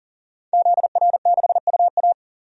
Morse code, heard as a keyed tone, about 40 wpm, 700 Hz. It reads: ZR6UA
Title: DXCC-100 Call Signs - 5 Characters - 40wpm